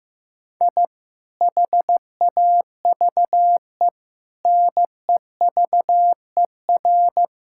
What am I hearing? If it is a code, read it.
Morse code, heard as a keyed tone, about 16 wpm, 700 Hz. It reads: I HAVE NEVER
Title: Sentences from Top 1000 Words - Review Part 5 - 15wpm